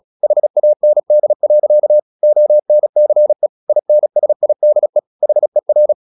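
Morse code, heard as a keyed tone, about 37 wpm, 600 Hz. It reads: HAND. ONCE INSIDE HER
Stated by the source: Morse practice at 36 wpm, text of Ken Follett's novel Never, from CW med Gnister.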